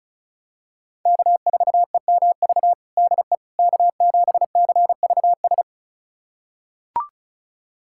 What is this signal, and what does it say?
Morse code, heard as a keyed tone, about 34 wpm, 700 Hz. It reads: K4EMV DE K7CVS
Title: USA Callsign DE Callsign - 5 Characters - 35wpm